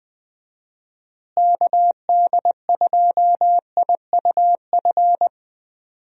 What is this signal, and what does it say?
Morse code, heard as a keyed tone, about 21 wpm, 700 Hz. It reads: KD2IUF
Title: USA Call Signs - 6 Characters - Repeated 2x - 20wpm